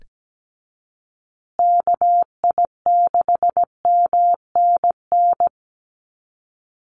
Morse code, heard as a keyed tone, about 17 wpm, 700 Hz. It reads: KI6MNN
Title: USA Call Signs Repeated 2x 17wpm